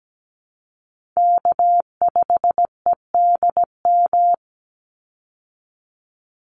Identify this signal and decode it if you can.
Morse code, heard as a keyed tone, about 17 wpm, 700 Hz. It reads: K5EDM
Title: Call Signs Repeated 3x - 17wpm